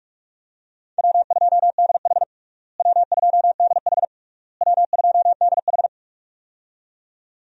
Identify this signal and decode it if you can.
Morse code, heard as a keyed tone, about 46 wpm, 700 Hz. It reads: W2DH W2DH W2DH
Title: USA Call Signs - 4 Characters - Repeated 3x - 45wpm